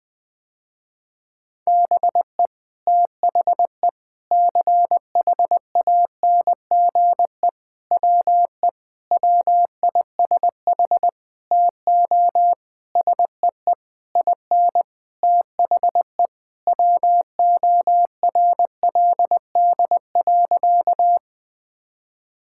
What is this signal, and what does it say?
Morse code, heard as a keyed tone, about 21 wpm, 700 Hz. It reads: BE THE CHANGE WE WISH TO SEE IN THE WORLD.
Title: Sentences with 9-Word Phrases - Spoken Beginning - 20wpm